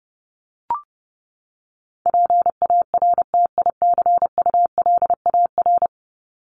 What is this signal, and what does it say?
Morse code, heard as a keyed tone, about 30 wpm, 700 Hz. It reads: PARTICULAR
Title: LY Suffix - Top 100 Words with Base Words 30wpm